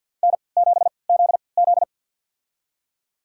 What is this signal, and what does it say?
Morse code, heard as a keyed tone, about 52 wpm, spaced as for 29 wpm, 700 Hz. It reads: N7ZZ
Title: USA Call Signs - 3x Character Spacing - 50wpm